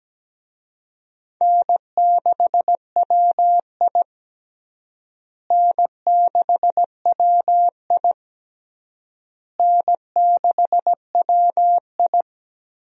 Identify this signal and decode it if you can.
Morse code, heard as a keyed tone, about 17 wpm, 700 Hz. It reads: N6WI N6WI N6WI
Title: USA Call Signs - 4 Characters - Repeated 3x - 17wpm